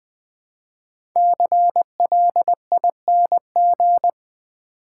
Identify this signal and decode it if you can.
Morse code, heard as a keyed tone, about 20 wpm, 700 Hz. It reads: CLING
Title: Build A Word - 1 Letter At A Time - 5-Letter Words - 20wpm